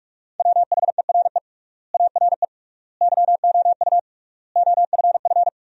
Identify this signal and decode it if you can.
Morse code, heard as a keyed tone, about 46 wpm, 700 Hz. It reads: WHERE ARE YOU OFF
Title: 3x Word-Spacing - Sentences from Top 400 Words - 45wpm